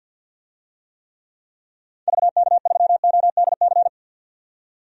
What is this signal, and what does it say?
Morse code, heard as a keyed tone, about 47 wpm, 700 Hz. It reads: VK3ODC
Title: DXCC-100 Call Signs - 6 Characters - 50wpm